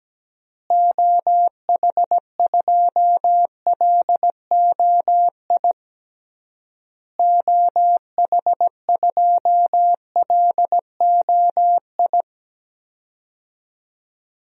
Morse code, heard as a keyed tone, about 17 wpm, 700 Hz. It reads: OH2LOI OH2LOI
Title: DXCC-100 Call Signs - 6 Characters - Repeated 3x - 17wpm